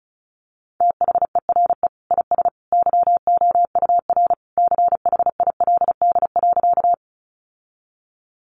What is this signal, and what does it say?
Morse code, heard as a keyed tone, about 34 wpm, 700 Hz. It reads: THERE IS YOUR CHILD.
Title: Sentences from Top 800 Words 35wpm